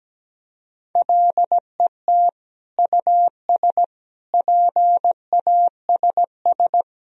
Morse code, heard as a keyed tone, about 17 wpm, 700 Hz. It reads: LET US PASS
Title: Sentences from Top 1000 Words - Review Part 1 - 17wpm